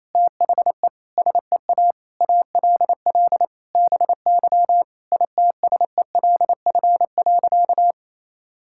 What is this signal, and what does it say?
Morse code, heard as a keyed tone, about 28 wpm, 700 Hz. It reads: THE SEA ALL BY ITSELF.
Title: Sentences from Top 1000 Words - Review Part 5 - 28wpm